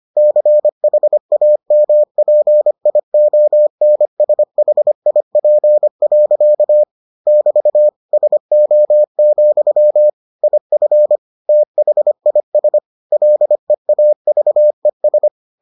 Morse code, heard as a keyed tone, about 25 wpm, 600 Hz. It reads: CHAMPIONSHIP. <BT> SO, IF THIS LEAVES